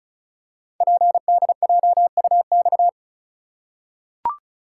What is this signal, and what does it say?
Morse code, heard as a keyed tone, about 35 wpm, 700 Hz. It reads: PDJUX
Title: Code Groups - 5 Letters - 35wpm